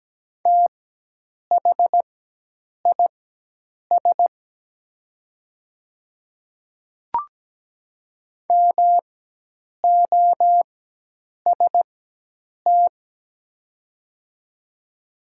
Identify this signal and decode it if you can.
Morse code, heard as a keyed tone, about 17 wpm, spaced as for 8 wpm, 700 Hz. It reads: THIS MOST
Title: 4x Character Spacing - Top 100 Words - 17wpm